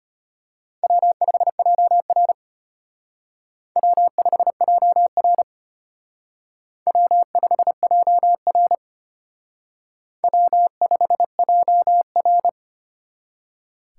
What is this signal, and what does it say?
Morse code, heard as a keyed tone, about 32 wpm, 700 Hz. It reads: W5JR W5JR W5JR W5JR